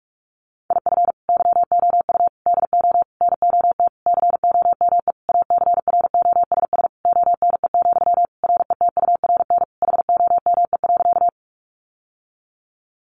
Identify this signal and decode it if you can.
Morse code, heard as a keyed tone, about 44 wpm, 700 Hz. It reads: IF YOU DO NOT COME ACROSS ONE, RETURN HOME.
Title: Sentences from Top 800 Words 45wpm